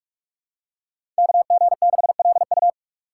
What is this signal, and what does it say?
Morse code, heard as a keyed tone, about 44 wpm, 700 Hz. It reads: KG6LU
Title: USA Call Signs - 5 Characters - 6x Character Spacing - 45wpm